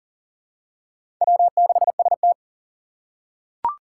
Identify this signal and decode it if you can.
Morse code, heard as a keyed tone, about 40 wpm, 700 Hz. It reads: W6ST
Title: USA Call Signs - 4 Characters - 3x Character Spacing - 40wpm